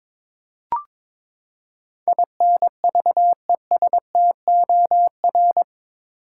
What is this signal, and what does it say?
Morse code, heard as a keyed tone, about 22 wpm, 700 Hz. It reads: INVESTOR
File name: practice / 22wpm / Eight Letter Words - 22wpm